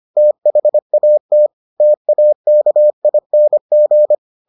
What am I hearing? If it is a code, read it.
Morse code, heard as a keyed tone, about 25 wpm, 600 Hz. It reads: THAT TAKING